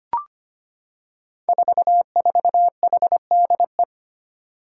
Morse code, heard as a keyed tone, about 25 wpm, 700 Hz. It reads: 44HDE
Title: Five Challenging Letter-Number Combinations 25wpm